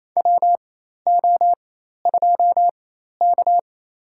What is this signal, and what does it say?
Morse code, heard as a keyed tone, about 29 wpm, spaced as for 13 wpm, 700 Hz. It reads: WO2K
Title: USA Call Signs - 4 Characters - 4x Character Spacing - 28wpm